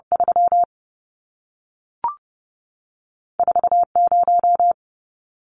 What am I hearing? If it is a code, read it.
Morse code, heard as a keyed tone, about 30 wpm, 700 Hz. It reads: 3 4 0
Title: Numbers - 2-Digit 30wpm